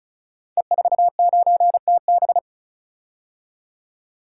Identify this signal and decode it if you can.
Morse code, heard as a keyed tone, about 36 wpm, 700 Hz. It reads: E49TB
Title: Code Groups - 5 Letter:Numbers - 35wpm